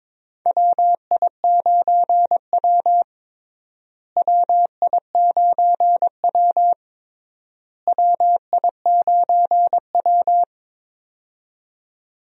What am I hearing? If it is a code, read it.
Morse code, heard as a keyed tone, about 22 wpm, 700 Hz. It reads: WI9W WI9W WI9W
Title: USA Call Signs - 4 Characters - Repeated 3x - 22wpm